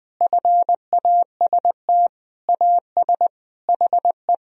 Morse code, heard as a keyed tone, about 20 wpm, 700 Hz. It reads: FAST AS HE
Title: Sentences from Top 1000 Words - Review Part 2 - 20wpm